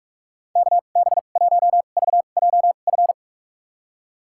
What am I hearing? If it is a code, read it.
Morse code, heard as a keyed tone, about 46 wpm, spaced as for 33 wpm, 700 Hz. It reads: KB1VJF
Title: USA Call Signs - 6 Characters - 2x Character Spacing - 45wpm